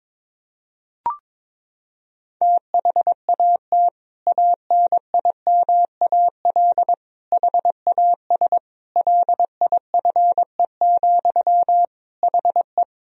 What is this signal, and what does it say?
Morse code, heard as a keyed tone, about 22 wpm, 700 Hz. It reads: THAT ANIMAL HAS LIFE, HE
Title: Sentences from Top 1000 Words - Review Part 1 - 22wpm